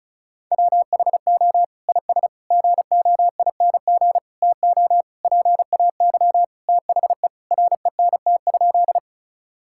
Morse code, heard as a keyed tone, about 34 wpm, 700 Hz. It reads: WHO IS GOING TO PAY THE RENT?
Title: Sentences from Top 1000 Words - Review Part 5 - 35wpm